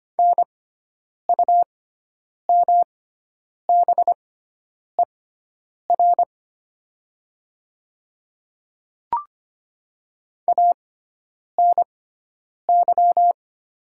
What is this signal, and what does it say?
Morse code, heard as a keyed tone, about 25 wpm, spaced as for 9 wpm, 700 Hz. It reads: NUMBER ANY
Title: 6x Character Spacing - Top 200 Words - 25wpm